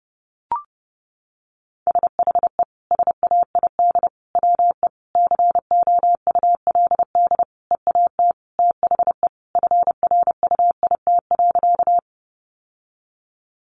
Morse code, heard as a keyed tone, about 30 wpm, 700 Hz. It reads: SHE SAID WE COULD EAT THE FRUIT.
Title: Sentences from Top 1300 Words 30wpm